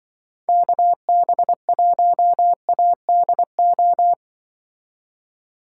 Morse code, heard as a keyed tone, about 24 wpm, 700 Hz. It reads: KB1ADO
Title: Speed Racing - USA 2x3 Call Signs 20wpm